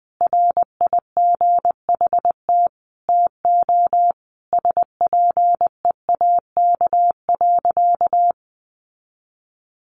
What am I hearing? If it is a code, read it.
Morse code, heard as a keyed tone, about 20 wpm, 700 Hz. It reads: RIGHT TO SPEAK.
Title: Sentences from Top 1300 Words 20wpm